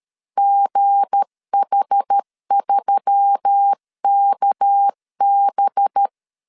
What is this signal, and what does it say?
Morse code, heard as a keyed tone, about 13 wpm, 800 Hz. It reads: GH3KB
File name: LEZIONI / 7 GF9Z / 7 GF9Z 12wpm 2147